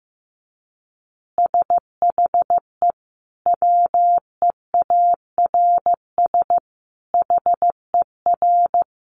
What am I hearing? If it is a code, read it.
Morse code, heard as a keyed tone, about 15 wpm, 700 Hz. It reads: SHE WEARS HER
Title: Easy Phrases 15wpm